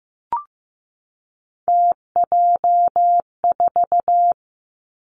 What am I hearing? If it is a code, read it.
Morse code, heard as a keyed tone, about 15 wpm, 700 Hz. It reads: TJ4
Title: Three Challenging Letter-Number Combinations 15wpm